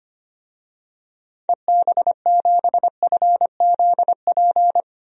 Morse code, heard as a keyed tone, about 25 wpm, 700 Hz. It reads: EB7FZP
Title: DXCC-100 Call Signs - 6 Characters - 25wpm